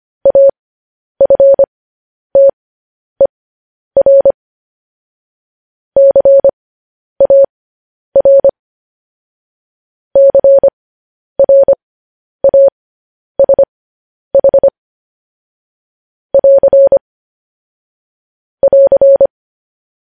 Morse code, heard as a keyed tone, about 25 wpm, spaced as for 10 wpm, 550 Hz. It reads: AFTER CAR CRASH <AR> <AR>
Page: news headlines in morse code at 10 wpm